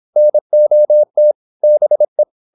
Morse code, heard as a keyed tone, about 26 wpm, 600 Hz. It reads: NOT BE
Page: Never 26wpm | CW med Gnister